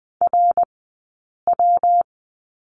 Morse code, heard as a keyed tone, about 20 wpm, spaced as for 8 wpm, 700 Hz. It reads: RW
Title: 197 Lesson 32d - Letter k - Two Characters 20wpm